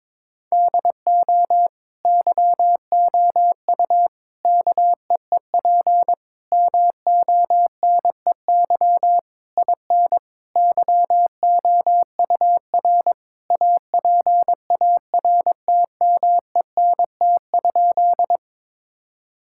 Morse code, heard as a keyed tone, about 22 wpm, 700 Hz. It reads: DO YOU KEEP MONEY IN YOUR APARTMENT?
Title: Sentences from Top 1000 Words - Review Part 6 - 22wpm